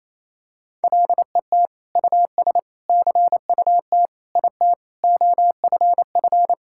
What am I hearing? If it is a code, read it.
Morse code, heard as a keyed tone, about 28 wpm, 700 Hz. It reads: LET US CUT IT OFF
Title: Sentences from Top 1000 Words - Review Part 6 - 28wpm